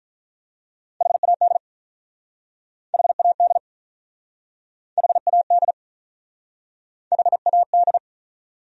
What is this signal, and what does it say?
Morse code, heard as a keyed tone, about 48 wpm, 700 Hz. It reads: HAD HAD HAD HAD